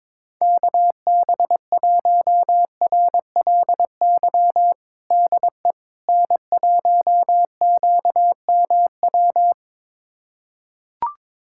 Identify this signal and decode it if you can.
Morse code, heard as a keyed tone, about 21 wpm, 700 Hz. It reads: KB1RLY DE N1QMW
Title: USA Callsign DE Callsign 22wpm